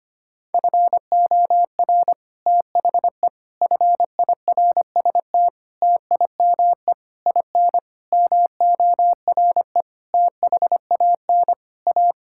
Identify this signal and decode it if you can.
Morse code, heard as a keyed tone, about 25 wpm, 700 Hz. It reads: FOR THE FIRST TIME IN MORE THAN A
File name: practice / 25wpm / Sentences with 9-Word Phrases - 25wpm